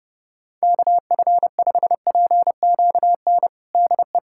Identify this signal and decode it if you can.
Morse code, heard as a keyed tone, about 32 wpm, 700 Hz. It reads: KF5PQN DE
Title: USA Callsign DE Callsign - 6 Characters - 30wpm